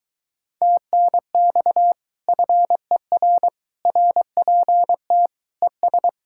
Morse code, heard as a keyed tone, about 23 wpm, 700 Hz. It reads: TNX FER RPT ES